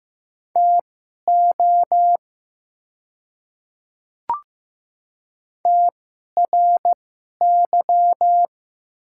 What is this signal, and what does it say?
Morse code, heard as a keyed tone, about 15 wpm, spaced as for 11 wpm, 700 Hz. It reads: TO TRY
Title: 2x Character Spacing - Top 200 Words - 15wpm